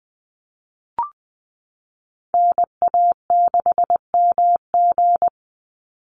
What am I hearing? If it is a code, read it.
Morse code, heard as a keyed tone, about 20 wpm, 700 Hz. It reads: NA6MG
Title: NA SOTA Activator Callsigns 20wpm